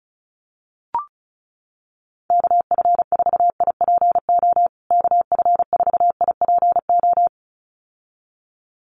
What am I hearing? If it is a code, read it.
Morse code, heard as a keyed tone, about 34 wpm, 700 Hz. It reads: KF4IPO KF4IPO
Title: USA Call Signs Repeated 2x 35wpm